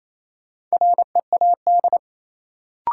Morse code, heard as a keyed tone, about 28 wpm, 700 Hz. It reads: READ
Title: Sentences from Top 200 Words - New Word Drill - 28wpm